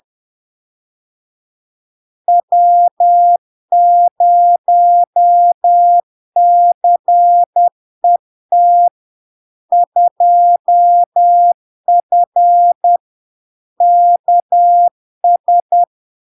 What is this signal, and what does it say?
Morse code, heard as a keyed tone, about 10 wpm, 700 Hz. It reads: W0CET 2F KS
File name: LARK-FD-2022-Log-in-CW-10wpm-700-Hz